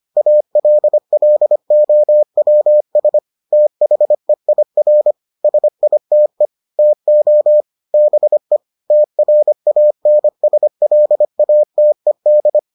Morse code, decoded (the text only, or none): ALLOWS THEIR SITE TO BE TRANSLATED